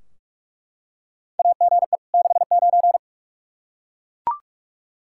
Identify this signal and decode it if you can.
Morse code, heard as a keyed tone, about 41 wpm, 700 Hz. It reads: AGE 69
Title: Practice QSO - Exchange 3 - More Info - 45wpm